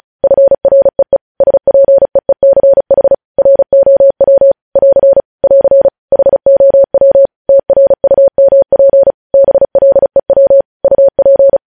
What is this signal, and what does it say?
Morse code, heard as a keyed tone, about 36 wpm, 550 Hz. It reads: FREE SPEECH ROW <AR> <AR> HOW TRUMP BLEW UP
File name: podcast / News headlines-1x-35x35